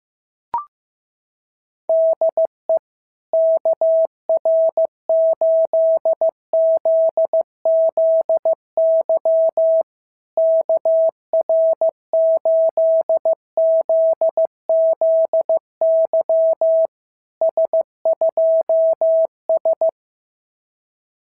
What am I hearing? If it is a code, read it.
Morse code, heard as a keyed tone, about 15 wpm, 650 Hz. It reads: DE KR8ZZY KR8ZZY S2S
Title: SOTA QSO Practice 15wpm